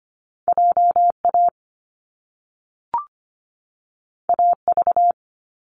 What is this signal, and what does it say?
Morse code, heard as a keyed tone, about 25 wpm, 700 Hz. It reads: JA AV
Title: Top Two Letter Combinations in Words 25wpm